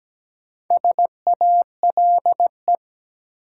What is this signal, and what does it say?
Morse code, heard as a keyed tone, about 17 wpm, 700 Hz. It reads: SALE
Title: Sentences from Top 1400 Words - New Word Drill Alt - 17wpm